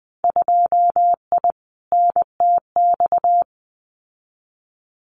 Morse code, heard as a keyed tone, about 20 wpm, 700 Hz. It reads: 2I NTX
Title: Winter Field Day Exchanges Encore 20wpm